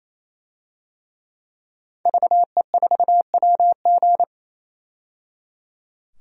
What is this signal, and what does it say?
Morse code, heard as a keyed tone, about 28 wpm, 700 Hz. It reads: VE4WG
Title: DXCC-100 Call Signs - 5 Characters - 28wpm